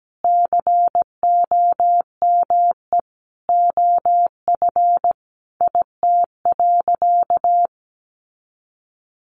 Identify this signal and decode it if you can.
Morse code, heard as a keyed tone, about 17 wpm, 700 Hz. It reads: COME OF IT.
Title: Sentences from Top 500 Words 17wpm